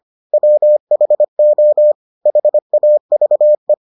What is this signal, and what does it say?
Morse code, heard as a keyed tone, about 25 wpm, 600 Hz. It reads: WHO HAVE